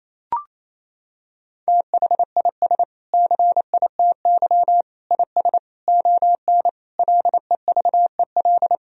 Morse code, heard as a keyed tone, about 27 wpm, 700 Hz. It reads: THIS CITY IS ON LEVEL
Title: Sentences from Top 1000 Words - Review Part 4 - 28wpm